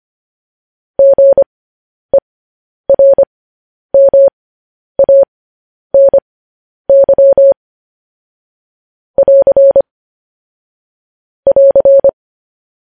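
Morse code, heard as a keyed tone, about 25 wpm, spaced as for 10 wpm, 550 Hz. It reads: GERMANY <AR> <AR>
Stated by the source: Morse code news headlines sent at 10 wpm.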